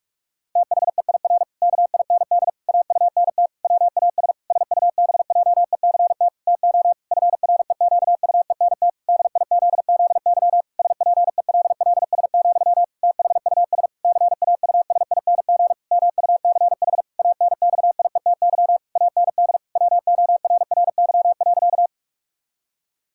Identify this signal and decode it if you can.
Morse code, heard as a keyed tone, about 46 wpm, 700 Hz. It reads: THEIR KIND AUNT WAS SUBJECT TO FREQUENT DIZZY SPELLS, THUS CAUSING MUCH ANXIETY AND WORRY.
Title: Pangram Challenge v2 - 45wpm